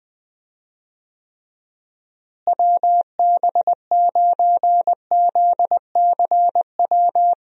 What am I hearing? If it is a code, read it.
Morse code, heard as a keyed tone, about 20 wpm, 700 Hz. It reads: WB9ZCW
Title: USA Call Signs - 6 Characters - Repeated 2x - 20wpm